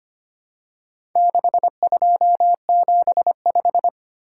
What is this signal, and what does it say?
Morse code, heard as a keyed tone, about 25 wpm, 700 Hz. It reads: 6275
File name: practice / 25wpm / Four Letter-Number Combinations - 25wpm